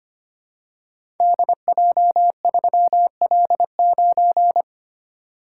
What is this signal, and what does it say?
Morse code, heard as a keyed tone, about 25 wpm, 690 Hz. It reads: DJ3L9
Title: Code Groups - 5 Letter:Numbers - 25wpm